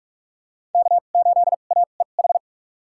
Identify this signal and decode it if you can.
Morse code, heard as a keyed tone, about 43 wpm, spaced as for 32 wpm, 700 Hz. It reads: K8AEH
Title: USA Call Signs - 5 Characters - 2x Character Spacing - 45wpm